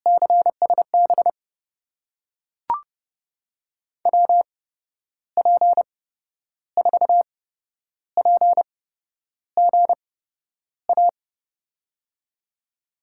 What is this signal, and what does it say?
Morse code, heard as a keyed tone, about 30 wpm, spaced as for 10 wpm, 700 Hz. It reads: CSB WP4PGA
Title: USA Call Signs - 6 Characters - 8x Character Spacing - 30wpm